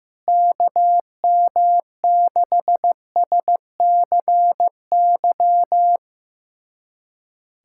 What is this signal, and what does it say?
Morse code, heard as a keyed tone, about 16 wpm, 700 Hz. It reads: KM6SCY
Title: USA Callsign DE Callsign 15wpm